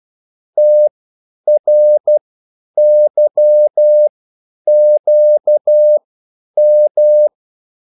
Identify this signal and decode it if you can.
Morse code, heard as a keyed tone, about 12 wpm, spaced as for 9 wpm, 600 Hz. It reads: TRYQM